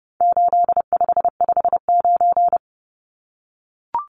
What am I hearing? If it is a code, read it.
Morse code, heard as a keyed tone, about 30 wpm, 700 Hz. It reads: 8559
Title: Numbers - 4-Digit 30wpm